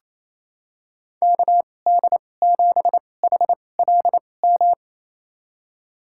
Morse code, heard as a keyed tone, about 28 wpm, spaced as for 20 wpm, 700 Hz. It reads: KD7HLM
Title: USA Call Signs - 6 Characters - 2x Character Spacing - 28wpm